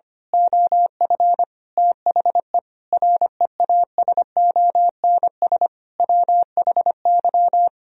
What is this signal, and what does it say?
Morse code, heard as a keyed tone, about 25 wpm, 700 Hz. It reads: OF THE REASONS WHY
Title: Speed-Racing - Sets of 5 Words 25wpm